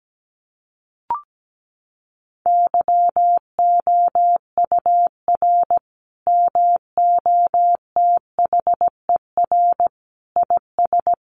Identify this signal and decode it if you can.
Morse code, heard as a keyed tone, about 17 wpm, 700 Hz. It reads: YOUR MOTHER IS
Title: Sentences from Top 1100 Words 17wpm